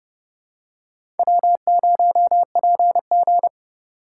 Morse code, heard as a keyed tone, about 30 wpm, 700 Hz. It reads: W0PG
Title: USA Call Signs - 4 Characters - 2x Character Spacing - 30wpm